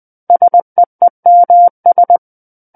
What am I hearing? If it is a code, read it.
Morse code, heard as a keyed tone, about 20 wpm, 700 Hz. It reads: SEEMS
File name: 20 WPM